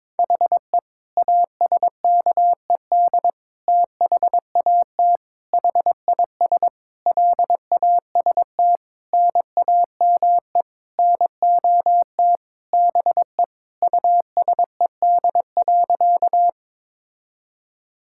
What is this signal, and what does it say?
Morse code, heard as a keyed tone, about 22 wpm, 700 Hz. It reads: HE ASKED THAT HIS LAST NAME NOT BE USED.
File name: Sentences with 9-Word Phrases - 22wpm